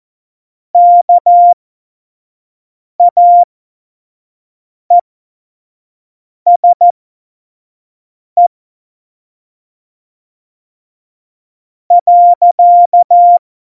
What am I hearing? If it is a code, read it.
Morse code, heard as a keyed tone, about 14 wpm, spaced as for 5 wpm, 700 Hz. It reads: KAESE .